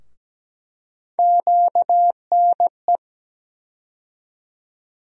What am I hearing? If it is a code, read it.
Morse code, heard as a keyed tone, about 17 wpm, 700 Hz. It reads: QNE
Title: USA Call Sign Postfixes - 3 Letters 17wpm